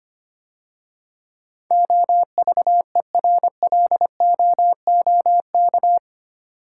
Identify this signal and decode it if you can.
Morse code, heard as a keyed tone, about 25 wpm, 700 Hz. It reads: OVERLOOK
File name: 8 Letter Words - First 3 Letters Given - 25wpm